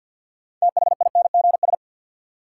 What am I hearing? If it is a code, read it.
Morse code, heard as a keyed tone, about 52 wpm, 700 Hz. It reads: THINGS
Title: Sentences from Top 1000 Words - Review Part 4 - New Word Drill - 50wpm